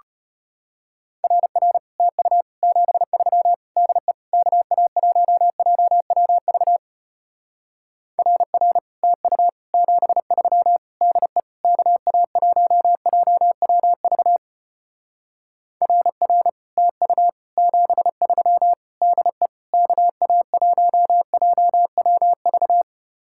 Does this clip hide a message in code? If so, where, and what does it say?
Yes: Morse code from 1.2 to 22.8 s, heard as a keyed tone, about 33 wpm, 700 Hz. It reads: RR TU 73 DE KA1JWV RR TU 73 DE KA1JWV RR TU 73 DE KA1JWV